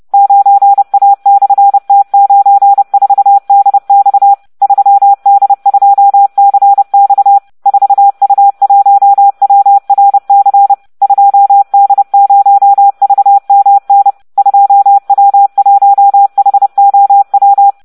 Morse code, heard as a keyed tone, about 30 wpm, 800 Hz. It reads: 9A/T94DX 3D2CX 4U1WRC 2D0VMN 2W1HOW